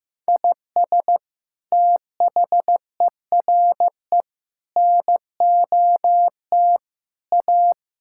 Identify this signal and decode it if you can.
Morse code, heard as a keyed tone, about 16 wpm, 700 Hz. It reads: IS THERE NOT A
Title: Sentences from Top 1600 Words 15wpm